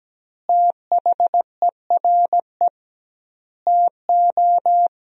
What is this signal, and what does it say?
Morse code, heard as a keyed tone, about 17 wpm, 700 Hz. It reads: THERE TO
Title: 2x Word-Spacing - Sentences from Top 100 Words - 17wpm